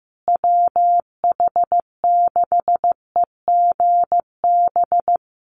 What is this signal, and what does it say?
Morse code, heard as a keyed tone, about 15 wpm, 700 Hz. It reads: WH6EGB
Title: USA Call Signs Repeated 2x 15wpm